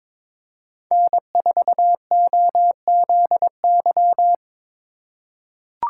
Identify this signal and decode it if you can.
Morse code, heard as a keyed tone, about 22 wpm, 700 Hz. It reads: N4OZY
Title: USA Call Signs - 5 Characters - 4x Character Spacing - 22wpm